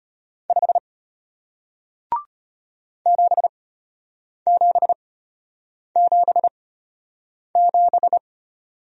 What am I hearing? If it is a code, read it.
Morse code, heard as a keyed tone, about 33 wpm, spaced as for 8 wpm, 700 Hz. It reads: 5 7777